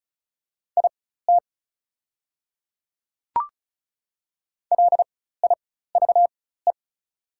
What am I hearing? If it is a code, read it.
Morse code, heard as a keyed tone, about 36 wpm, spaced as for 16 wpm, 700 Hz. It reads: IT LIVE
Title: 4x Character Spacing - Top 200 Words - 35wpm